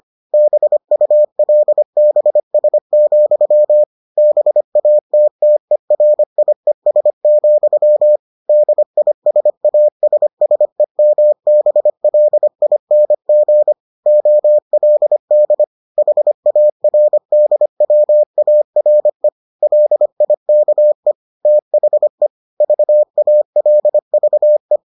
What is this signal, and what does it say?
Morse code, heard as a keyed tone, about 25 wpm, 600 Hz. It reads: BULBS, BATTERIES, DISASSEMBLING OLD HARDWARE LIKE THE VALVE